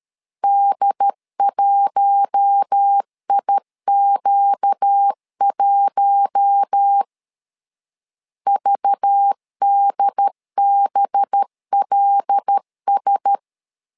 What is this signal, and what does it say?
Morse code, heard as a keyed tone, about 13 wpm, 800 Hz. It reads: D1IQ1 VDBLS
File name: LEZIONI / 5 LBVC / 5 LBV 12wpm 1855